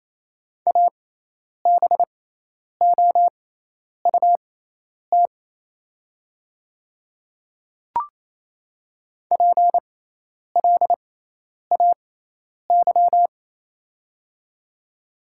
Morse code, heard as a keyed tone, about 28 wpm, spaced as for 10 wpm, 700 Hz. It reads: ABOUT PLAY
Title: 6x Character Spacing - Top 500 Words - 28wpm